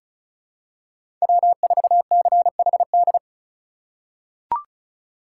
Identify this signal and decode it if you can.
Morse code, heard as a keyed tone, about 34 wpm, 700 Hz. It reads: W4CHD
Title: USA Call Signs - 5 Characters - 3x Character Spacing - 35wpm